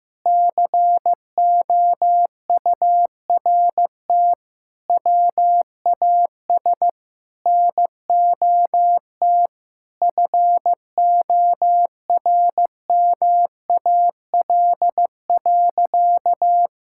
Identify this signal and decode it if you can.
Morse code, heard as a keyed tone, about 15 wpm, 700 Hz. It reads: COURT WAS NOT FORMAL.
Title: Sentences from Top 1600 Words 15wpm